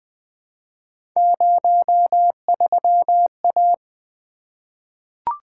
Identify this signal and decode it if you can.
Morse code, heard as a keyed tone, about 20 wpm, 700 Hz. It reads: 03A
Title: Three Letter-Number Combinations - 20wpm